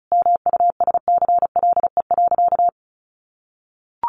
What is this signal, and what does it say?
Morse code, heard as a keyed tone, about 34 wpm, 700 Hz. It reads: MUSCLE.
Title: Sentences from Top 1300 Words 35wpm